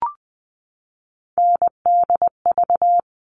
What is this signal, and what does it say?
Morse code, heard as a keyed tone, about 20 wpm, 700 Hz. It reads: NDV
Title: All Cut Numbers - 3-Digit 20wpm